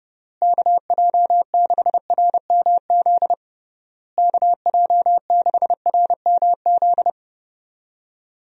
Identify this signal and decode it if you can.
Morse code, heard as a keyed tone, about 30 wpm, 700 Hz. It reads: KJ6RMZ KJ6RMZ
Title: USA Call Signs - 6 Characters - Repeated 3x - 30wpm